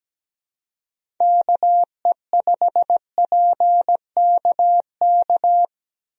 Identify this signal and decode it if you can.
Morse code, heard as a keyed tone, about 17 wpm, 700 Hz. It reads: KE5PKK
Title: USA Call Signs - 6 Characters - Repeated 2x - 17wpm